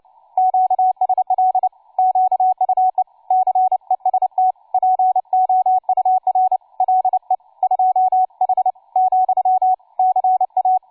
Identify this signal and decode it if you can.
Morse code, heard as a keyed tone, about 29 wpm, 750 Hz, with radio noise. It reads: QSL QUE CEST POUR LE 2H , CA
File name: cw forum / tu2 f5ldy 18mhz 16h46